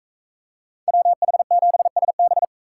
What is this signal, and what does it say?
Morse code, heard as a keyed tone, about 40 wpm, 700 Hz. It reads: WH7SB